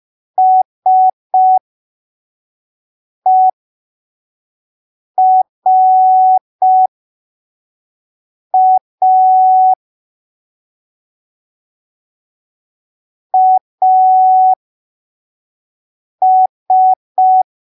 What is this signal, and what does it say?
Morse code, heard as a keyed tone, about 5 wpm, spaced as for 3 wpm, 750 Hz. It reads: SERA AS
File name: arquivos / texto / texto 30 05ppm